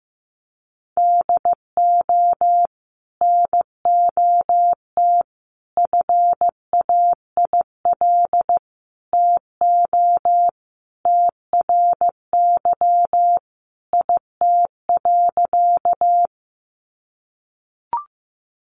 Morse code, heard as a keyed tone, about 14 wpm, 700 Hz. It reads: DO NOT FAIL TO TRY IT.
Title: Sentences from Top 1100 Words 15wpm